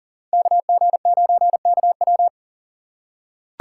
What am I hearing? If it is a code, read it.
Morse code, heard as a keyed tone, about 40 wpm, 700 Hz. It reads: KG9KW
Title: USA Call Signs - 5 Characters - 2x Character Spacing - 40wpm